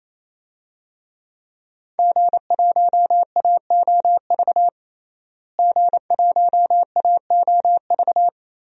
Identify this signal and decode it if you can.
Morse code, heard as a keyed tone, about 27 wpm, 700 Hz. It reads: G1AOV G1AOV
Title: DXCC-100 Call Signs - 5 Characters - Repeated 3x - 28wpm